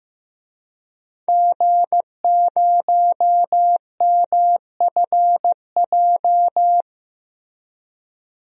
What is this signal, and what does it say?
Morse code, heard as a keyed tone, about 15 wpm, 700 Hz. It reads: G0MFJ
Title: DXCC-100 Call Signs - 5 Characters - Repeated 2x - 15wpm